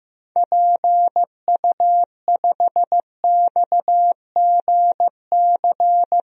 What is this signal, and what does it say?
Morse code, heard as a keyed tone, about 15 wpm, 700 Hz. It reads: PU5XGC
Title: DXCC-100 Call Signs - 6 Characters - 15wpm